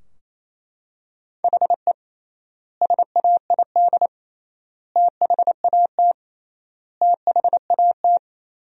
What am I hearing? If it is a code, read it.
Morse code, heard as a keyed tone, about 28 wpm, 700 Hz. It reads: HE SAID THAT THAT
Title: 3x Word-Spacing - Sentences from Top 200 Words - 28wpm